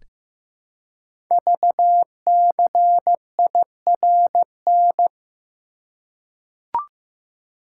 Morse code, heard as a keyed tone, about 15 wpm, 700 Hz. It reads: VCIRN
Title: Code Groups - 5 Letters - 15wpm